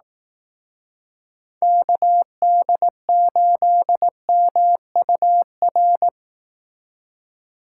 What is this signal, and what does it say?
Morse code, heard as a keyed tone, about 18 wpm, 700 Hz. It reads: KD8MUR
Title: Speed Racing - USA Call Signs 15wpm